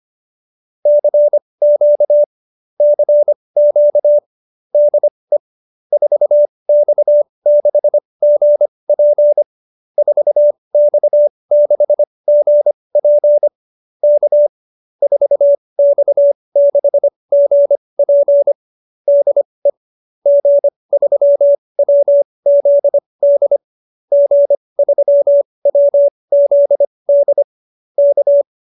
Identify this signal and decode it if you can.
Morse code, heard as a keyed tone, about 25 wpm, spaced as for 20 wpm, 590 Hz. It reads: CQ CQ DE 4X6GP 4X6GP K 4X6GP DE G3WZD G3WZD K
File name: QSO310 20